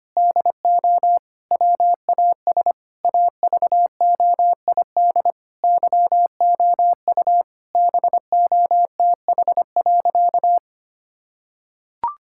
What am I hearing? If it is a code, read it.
Morse code, heard as a keyed tone, about 25 wpm, 700 Hz. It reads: DO WAS AVOID YOU BOTH.
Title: Sentences from Top 1000 Words - Review Part 6 - 25wpm